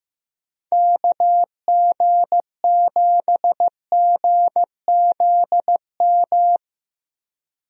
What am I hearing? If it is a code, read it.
Morse code, heard as a keyed tone, about 15 wpm, 700 Hz. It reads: KG7GZM
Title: USA Call Signs - 6 Characters - Repeated 2x - 15wpm